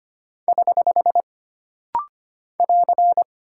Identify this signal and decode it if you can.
Morse code, heard as a keyed tone, about 25 wpm, 700 Hz. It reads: <HH> <AR>
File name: Single Punctuation-Prosign - Rapid-Fire - 25wpm